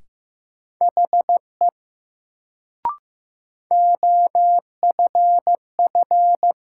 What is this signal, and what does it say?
Morse code, heard as a keyed tone, about 15 wpm, 700 Hz. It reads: HE OFF
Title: Sets of 2 Words - New Word Drill - 15wpm